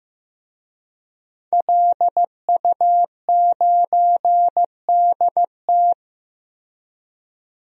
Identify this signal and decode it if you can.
Morse code, heard as a keyed tone, about 15 wpm, 700 Hz. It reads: LU9DT
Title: DXCC-100 Call Signs - 5 Characters - Repeated 2x - 15wpm